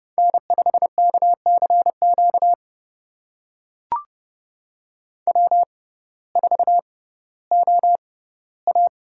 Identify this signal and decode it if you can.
Morse code, heard as a keyed tone, about 31 wpm, spaced as for 17 wpm, 700 Hz. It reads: N5KCQ W4OA
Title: USA Call Signs - 6x Character Spacing - 30wpm